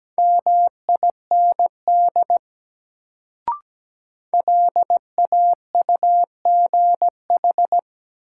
Morse code, heard as a keyed tone, about 17 wpm, 700 Hz. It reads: MIND LAUGH
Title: Sentences from Top 500 Words - New Word Drill - 17wpm